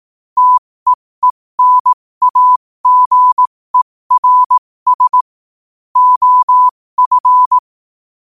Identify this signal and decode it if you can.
Morse code, heard as a keyed tone, about 18 wpm, spaced as for 15 wpm, 1000 Hz. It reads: TEENAGERS OF
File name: qotdfiles / 15wpm